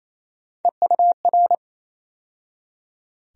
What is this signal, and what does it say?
Morse code, heard as a keyed tone, about 28 wpm, 700 Hz. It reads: EUR
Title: USA Call Sign Postfixes - 3 Letters 28wpm